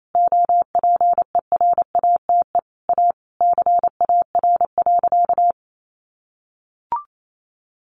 Morse code, heard as a keyed tone, about 28 wpm, 700 Hz. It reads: OPERATE A CAR.
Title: Sentences from Top 1500 Words 28wpm